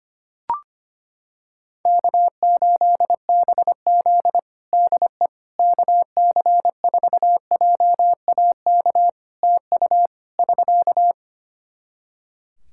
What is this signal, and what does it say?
Morse code, heard as a keyed tone, about 25 wpm, 700 Hz. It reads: K8BZ DE KC4JAK TU <SK>
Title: Practice QSO - Exchange 4 - Ending - 25wpm